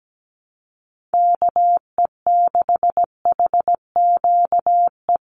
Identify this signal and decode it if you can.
Morse code, heard as a keyed tone, about 17 wpm, 700 Hz. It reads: KE6HQE
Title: USA Call Signs 17wpm